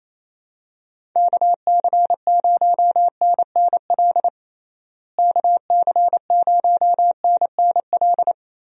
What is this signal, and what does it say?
Morse code, heard as a keyed tone, about 28 wpm, 700 Hz. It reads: KC0NNL KC0NNL
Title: USA Call Signs - 6 Characters - Repeated 3x - 28wpm